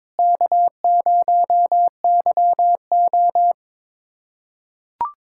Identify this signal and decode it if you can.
Morse code, heard as a keyed tone, about 22 wpm, 700 Hz. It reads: K0YO
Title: USA Callsign DE Callsign - 4 Characters - 22wpm